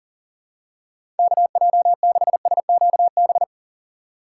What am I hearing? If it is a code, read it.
Morse code, heard as a keyed tone, about 40 wpm, 700 Hz. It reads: KJ6SQB
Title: USA Call Signs - 6 Characters - 3x Character Spacing - 40wpm